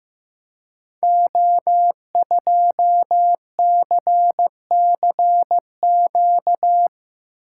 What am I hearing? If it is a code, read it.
Morse code, heard as a keyed tone, about 15 wpm, 700 Hz. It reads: O2CCQ
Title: Code Groups - 5 Letter:Numbers - 15wpm